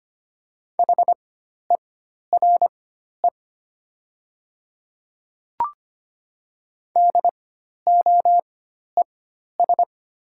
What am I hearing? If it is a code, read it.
Morse code, heard as a keyed tone, about 25 wpm, spaced as for 12 wpm, 700 Hz. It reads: HERE DOES
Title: 4x Character Spacing - Top 200 Words - 25wpm